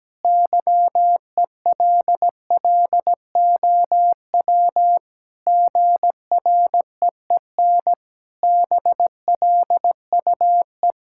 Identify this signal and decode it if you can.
Morse code, heard as a keyed tone, about 17 wpm, 700 Hz. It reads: YELLOW GREEN BLUE
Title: Pangram Challenge v2 - 17wpm